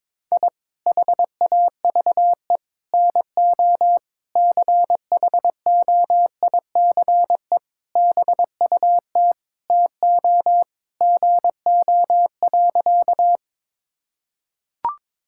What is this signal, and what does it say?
Morse code, heard as a keyed tone, about 21 wpm, 700 Hz. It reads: I HAVE NO CHOICE BUT TO GO.
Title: Sentences from Top 1000 Words - Review Part 6 - 22wpm